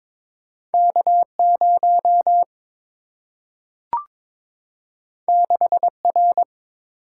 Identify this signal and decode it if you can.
Morse code, heard as a keyed tone, about 22 wpm, 700 Hz. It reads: K 0 6 R
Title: Single-Letter Single-Number 22wpm